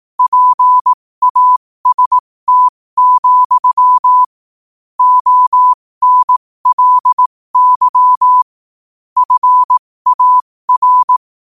Morse code, heard as a keyed tone, about 18 wpm, spaced as for 15 wpm, 1000 Hz. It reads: PAST, ONLY FAR
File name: qotdfiles / 15wpm